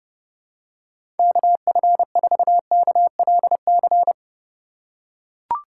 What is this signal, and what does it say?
Morse code, heard as a keyed tone, about 30 wpm, 700 Hz. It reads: KF4KLC
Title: USA Call Signs - 4x Character Spacing - 30wpm